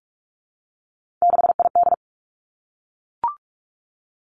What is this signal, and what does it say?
Morse code, heard as a keyed tone, about 46 wpm, 700 Hz. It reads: 6ID
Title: Three Challenging Letter-Number Combinations 45wpm